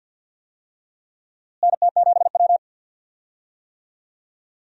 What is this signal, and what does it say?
Morse code, heard as a keyed tone, about 52 wpm, 700 Hz. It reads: NT7W
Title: USA Call Signs - 4 Characters - Repeated 2x - 50wpm